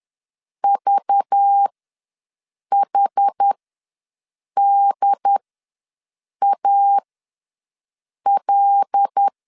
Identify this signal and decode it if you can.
Morse code, heard as a keyed tone, about 10 wpm, spaced as for 6 wpm, 800 Hz. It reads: VHDAL
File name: LEZIONI / 5 LBVC / 5 LBV ws 10wpm 3645